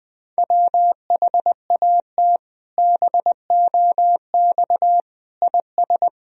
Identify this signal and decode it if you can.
Morse code, heard as a keyed tone, about 20 wpm, 700 Hz. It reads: WHAT BOX IS
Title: Sentences from Top 1000 Words - Review Part 2 - 20wpm